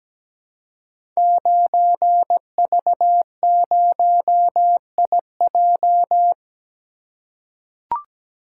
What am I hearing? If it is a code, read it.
Morse code, heard as a keyed tone, about 17 wpm, 700 Hz. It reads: 9V0IJ
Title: Code Groups - 5 Letter:Numbers - 17wpm